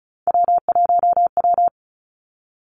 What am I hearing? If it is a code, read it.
Morse code, heard as a keyed tone, about 36 wpm, 700 Hz. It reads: W1W
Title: Three Challenging Letter-Number Combinations 35wpm